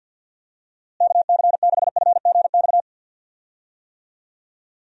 Morse code, heard as a keyed tone, about 49 wpm, 700 Hz. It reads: KC6FGX
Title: USA Call Signs - 6 Characters - Repeated 3x - 50wpm